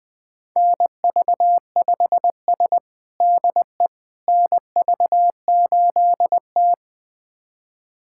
Morse code, heard as a keyed tone, about 20 wpm, 700 Hz. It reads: NV5S DE NV8T
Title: USA Callsign DE Callsign - 4 Characters - 20wpm